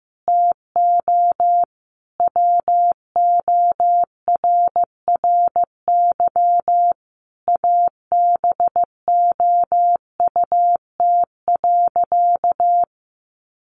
Morse code, heard as a keyed tone, about 15 wpm, 700 Hz. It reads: TO WORRY ABOUT.
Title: Sentences from Top 1100 Words 15wpm